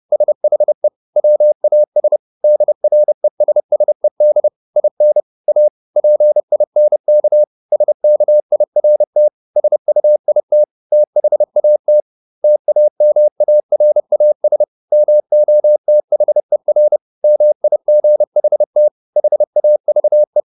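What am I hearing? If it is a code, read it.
Morse code, heard as a keyed tone, about 30 wpm, 600 Hz. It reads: SHE WAS DRESSED IN A PINK SKIRT SUIT THAT TAMARAS MOTHER MIGHT HAVE
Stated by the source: Morse code at 30 wpm, the novel Never by Ken Follett in English, from CW med Gnister.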